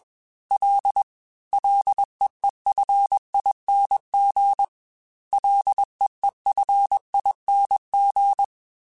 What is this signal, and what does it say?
Morse code, heard as a keyed tone, about 21 wpm, 800 Hz. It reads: L LEEFING LEEFING
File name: F and L / F and L mp3 / F and L Prefixes 3x2 Twice @20